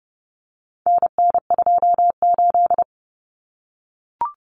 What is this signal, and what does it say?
Morse code, heard as a keyed tone, about 30 wpm, 700 Hz. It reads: NN28
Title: Common Cut Numbers - 4-Digit 30wpm